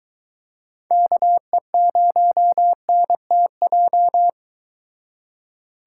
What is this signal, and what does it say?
Morse code, heard as a keyed tone, about 23 wpm, 700 Hz. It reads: KE0NTJ